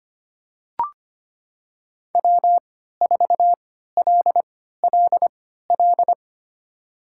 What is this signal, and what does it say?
Morse code, heard as a keyed tone, about 25 wpm, spaced as for 14 wpm, 700 Hz. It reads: W4LLL
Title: USA Call Signs - 5 Characters - 3x Character Spacing - 25wpm